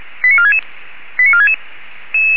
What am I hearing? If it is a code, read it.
Five-tone selective call (ZVEI), heard as quick runs of five tones, tones of about 70 ms. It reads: 89466 89466